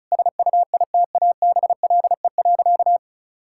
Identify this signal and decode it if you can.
Morse code, heard as a keyed tone, about 36 wpm, 700 Hz. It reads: SUITABLE.
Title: Sentences from Top 1600 Words 35wpm